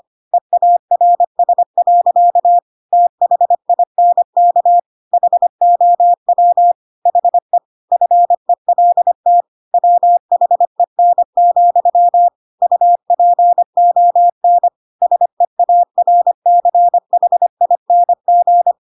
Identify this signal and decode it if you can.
Morse code, heard as a keyed tone, about 25 wpm, 700 Hz. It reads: EARS. THINK HOW HE FELT WHEN, UPON SEARCHING
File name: Pinocchio / Pinocchio ch3 25 25 ik1hge